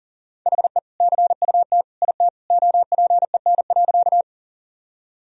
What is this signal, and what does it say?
Morse code, heard as a keyed tone, about 40 wpm, 700 Hz. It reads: HE CUT IT OPEN.
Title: Sentences from Top 1000 Words - Review Part 1 - 40wpm